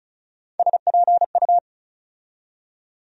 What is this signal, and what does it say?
Morse code, heard as a keyed tone, about 36 wpm, 700 Hz. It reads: SPU
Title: USA Call Sign Postfixes - 3 Letters 35wpm